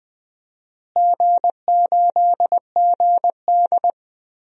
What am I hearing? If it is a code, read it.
Morse code, heard as a keyed tone, about 20 wpm, 700 Hz. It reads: G8GD
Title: DXCC-100 Call Signs - 4 Characters - Repeated 3x - 20wpm